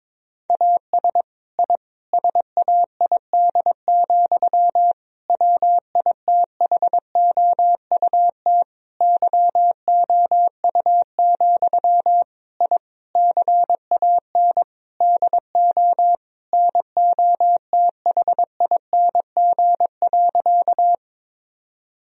Morse code, heard as a keyed tone, about 22 wpm, 700 Hz. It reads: AS I SAID, WITHOUT YOU, I CAN DO NOTHING.
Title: Sentences from Top 1000 Words - Review Part 3 - 22wpm